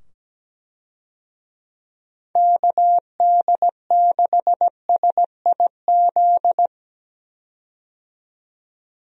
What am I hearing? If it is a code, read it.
Morse code, heard as a keyed tone, about 17 wpm, 700 Hz. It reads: KD6SIZ
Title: USA Call Signs - 6 Characters - 17wpm